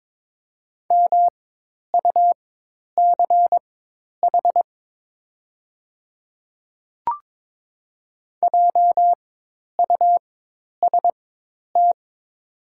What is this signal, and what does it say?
Morse code, heard as a keyed tone, about 21 wpm, spaced as for 10 wpm, 700 Hz. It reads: MUCH JUST
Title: 4x Character Spacing - Top 200 Words - 22wpm